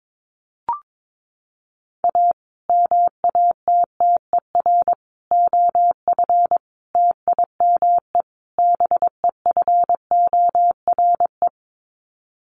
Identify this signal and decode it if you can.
Morse code, heard as a keyed tone, about 22 wpm, 700 Hz. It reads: A MATTER OF TIME BEFORE
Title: Sets of 5 Words Encore 22wpm